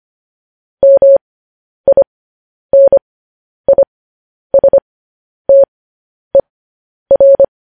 Morse code, heard as a keyed tone, about 25 wpm, spaced as for 10 wpm, 550 Hz. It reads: MINISTER
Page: news-headlines-in-morse-code-at-10-wpm